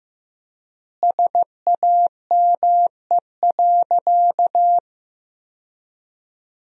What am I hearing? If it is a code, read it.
Morse code, heard as a keyed tone, about 15 wpm, 700 Hz. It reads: SAME.
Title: 4x Word-Spacing - Sentences from Top 300 Words - 15wpm